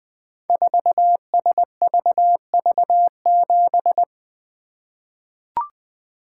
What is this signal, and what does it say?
Morse code, heard as a keyed tone, about 20 wpm, 700 Hz. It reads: 4SVV7
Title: Code Groups - 5 Letter:Numbers - 20wpm